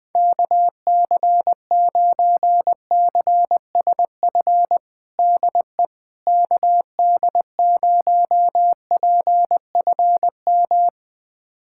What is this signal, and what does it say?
Morse code, heard as a keyed tone, about 20 wpm, 700 Hz. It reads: KC9CSF DE KD0PFM
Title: USA Callsign DE Callsign - 6 Characters - 20wpm